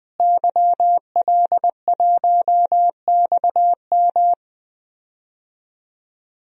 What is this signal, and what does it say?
Morse code, heard as a keyed tone, about 20 wpm, 700 Hz. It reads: YL1XM
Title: DXCC-100 Call Signs - 5 Characters - Repeated 2x - 20wpm